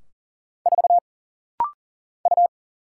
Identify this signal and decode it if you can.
Morse code, heard as a keyed tone, about 40 wpm, 700 Hz. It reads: 4 U
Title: Single Letter-Number Alt - Rapid-Fire 40wpm